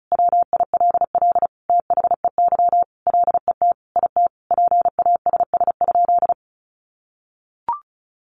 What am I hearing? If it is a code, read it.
Morse code, heard as a keyed tone, about 34 wpm, 700 Hz. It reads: WILL THEY LET IT PASS?
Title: Sentences from Top 400 Words 35wpm